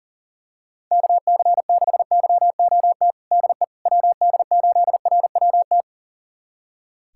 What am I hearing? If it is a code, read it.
Morse code, heard as a keyed tone, about 40 wpm, 700 Hz. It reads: KC6YOT DE WD8RWT
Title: USA Callsign DE Callsign 40wpm